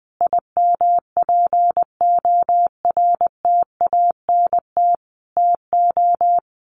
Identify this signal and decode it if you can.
Morse code, heard as a keyed tone, about 20 wpm, 700 Hz. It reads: IMPORTANT TO
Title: Sets of 4 Words 20wpm